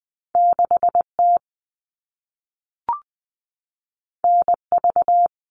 Morse code, heard as a keyed tone, about 20 wpm, 700 Hz. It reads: 6T NV
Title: All Cut Numbers - 2-Digit 20wpm